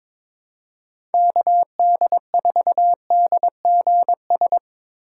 Morse code, heard as a keyed tone, about 22 wpm, 700 Hz. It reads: KD4DGS
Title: USA Call Signs - 4x Character Spacing - 22wpm